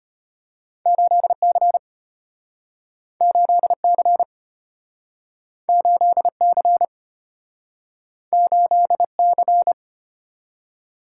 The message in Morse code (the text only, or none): 8C 8C 8C 8C